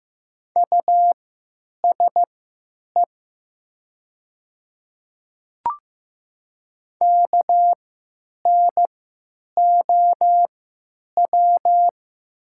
Morse code, heard as a keyed tone, about 15 wpm, spaced as for 9 wpm, 700 Hz. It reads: USE KNOW
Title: 3x Character Spacing - Top 200 Words - 15wpm